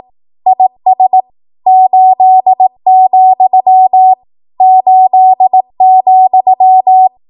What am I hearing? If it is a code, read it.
Morse code, heard as a keyed tone, about 18 wpm, 750 Hz. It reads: IS 8, 8,
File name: Morse / 18WPM